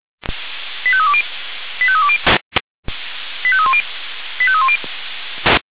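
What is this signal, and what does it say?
Five-tone selective call (ZVEI), heard as quick runs of five tones, tones of about 70 ms. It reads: 85320 85320 85310 85310
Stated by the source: ZVEI five-tone selective call received by SDR, feuerwehr-orlamuende.org fire brigade audio archive.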